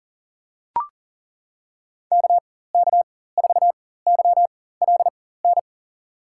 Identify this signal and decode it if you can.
Morse code, heard as a keyed tone, about 40 wpm, spaced as for 19 wpm, 700 Hz. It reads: KK4YLN
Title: USA Call Signs - 6 Characters - 4x Character Spacing - 40wpm